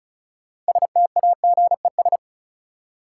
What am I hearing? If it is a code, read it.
Morse code, heard as a keyed tone, about 36 wpm, 700 Hz. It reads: STAGES
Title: Sets of 5 Words Encore - New Word Drill - 35wpm